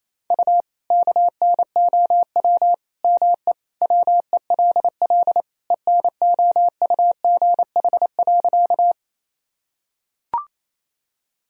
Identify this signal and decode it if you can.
Morse code, heard as a keyed tone, about 28 wpm, 700 Hz. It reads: U KNOW ME WELL ENOUGH.
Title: Sentences from Top 400 Words and Abbreviations 28wpm